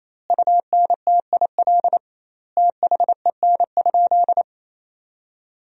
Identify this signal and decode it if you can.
Morse code, heard as a keyed tone, about 28 wpm, 700 Hz. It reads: UNTIL THEN?
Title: 2x Word-Spacing - Sentences from Top 500 Words - 28wpm